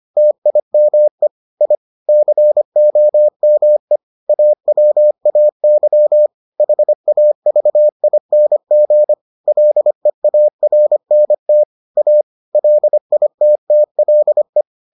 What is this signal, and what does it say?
Morse code, heard as a keyed tone, about 25 wpm, 600 Hz. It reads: TIME I COME AWAY HAVING LEARNT A LITTLE